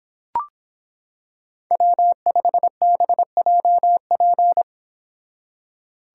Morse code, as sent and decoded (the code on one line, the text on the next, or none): .-- ..... -... .--- .--.
W5BJP